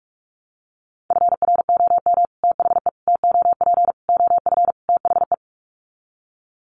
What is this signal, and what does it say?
Morse code, heard as a keyed tone, about 44 wpm, 700 Hz. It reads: FROM THE TOP OF THE
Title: Sets of 5 Words Encore 45wpm